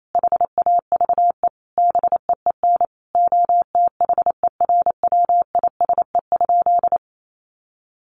Morse code, heard as a keyed tone, about 28 wpm, 700 Hz. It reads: HAVE BEEN OTHERWISE?
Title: Sentences from Top 1200 Words 28wpm